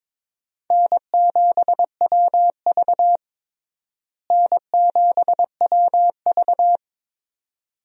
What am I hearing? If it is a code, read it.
Morse code, heard as a keyed tone, about 22 wpm, 700 Hz. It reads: N7WV N7WV
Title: USA Call Signs - 4 Characters - Repeated 2x - 22wpm